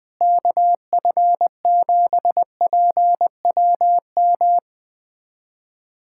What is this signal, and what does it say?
Morse code, heard as a keyed tone, about 20 wpm, 700 Hz. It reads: KF7PWM
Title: USA Callsign DE Callsign 20wpm